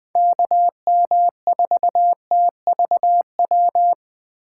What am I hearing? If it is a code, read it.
Morse code, heard as a keyed tone, about 21 wpm, 700 Hz. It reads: KM4TVW